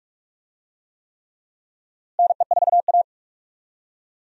Morse code, heard as a keyed tone, about 46 wpm, 700 Hz. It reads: NE4A
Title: USA Call Signs - 4 Characters - Repeated 3x - 45wpm